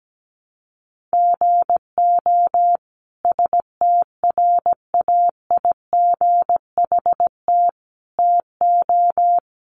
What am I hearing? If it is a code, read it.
Morse code, heard as a keyed tone, about 17 wpm, 700 Hz. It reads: GO STRAIGHT TO
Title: Sentences from Top 1000 Words 17wpm